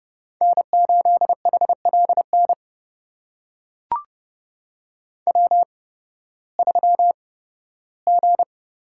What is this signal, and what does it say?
Morse code, heard as a keyed tone, about 30 wpm, spaced as for 16 wpm, 700 Hz. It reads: N8HLN W3G
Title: USA Call Signs - 8x Character Spacing - 30wpm